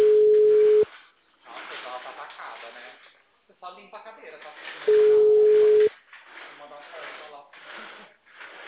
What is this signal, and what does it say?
Ringback tone (Europe, 425 Hz), with speech.